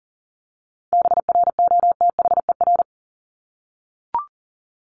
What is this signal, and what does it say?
Morse code, heard as a keyed tone, about 40 wpm, 700 Hz. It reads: BROTHER
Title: Top 1751-2000 Words 40wpm